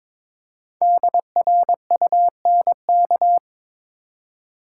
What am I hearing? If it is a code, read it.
Morse code, heard as a keyed tone, about 23 wpm, 700 Hz. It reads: DRUNK
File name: 5 Letter Words - First 3 Letters Given - 22wpm